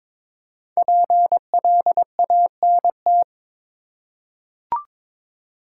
Morse code, heard as a keyed tone, about 23 wpm, 700 Hz. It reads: PLANT
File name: 5 Letter Words - First 3 Letters Given - 22wpm